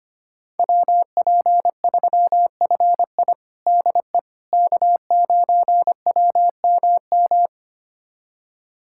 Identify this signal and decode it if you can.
Morse code, heard as a keyed tone, about 25 wpm, 700 Hz. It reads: WP3FI DE K9WMM
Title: USA Callsign DE Callsign - 5 Characters - 25wpm